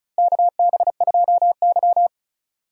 Morse code, heard as a keyed tone, about 36 wpm, 700 Hz. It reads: KB2Y
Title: USA Call Signs - 4 Characters - Repeated 3x - 35wpm